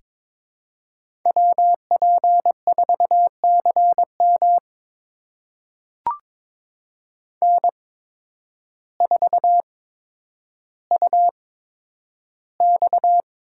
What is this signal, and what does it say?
Morse code, heard as a keyed tone, about 22 wpm, spaced as for 10 wpm, 700 Hz. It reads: WP4CM N4UX
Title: USA Call Signs - 5 Characters - 8x Character Spacing - 22wpm